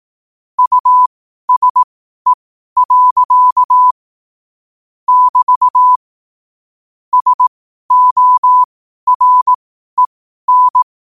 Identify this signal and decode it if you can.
Morse code, heard as a keyed tone, about 18 wpm, spaced as for 12 wpm, 1000 Hz. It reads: USE. <BT> SOREN
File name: qotdfiles / 12wpm